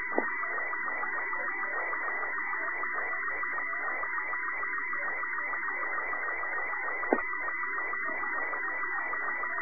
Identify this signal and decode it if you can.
SSTV (slow-scan television), Robot 36.
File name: sat / RADIOSCAF-B 24Dec2011